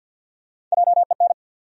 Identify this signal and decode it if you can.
Morse code, heard as a keyed tone, about 53 wpm, 700 Hz. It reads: JEN
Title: USA Common Names - Max 4 Letters - 50wpm